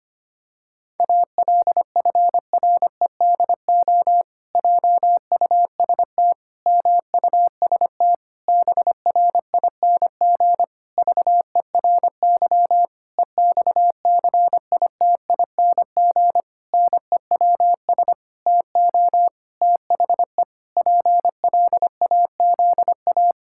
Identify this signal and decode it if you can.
Morse code, heard as a keyed tone, about 25 wpm, 700 Hz. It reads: ALFREDO JUST MUST BRING VERY EXCITING NEWS TO THE PLAZA
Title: Pangram Challenge v2 - 25wpm